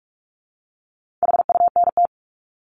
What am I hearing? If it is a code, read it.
Morse code, heard as a keyed tone, about 44 wpm, 700 Hz. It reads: HUNT
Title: Top 1501-1750 Words 45wpm